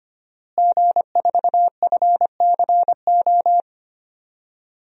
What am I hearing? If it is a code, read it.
Morse code, heard as a keyed tone, about 25 wpm, 700 Hz. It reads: G4FCO
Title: DXCC-100 Call Signs - 5 Characters - Repeated 3x - 25wpm